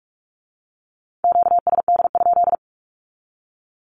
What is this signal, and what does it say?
Morse code, heard as a keyed tone, about 46 wpm, 700 Hz. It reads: QSD?
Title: Q-Codes with Spoken Meaning - 45wpm